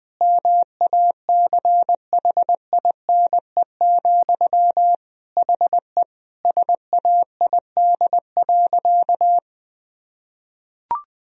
Morse code, heard as a keyed tone, about 20 wpm, 700 Hz. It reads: MACHINE, HE SAID.
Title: Sentences from Top 1000 Words - Review Part 2 - 20wpm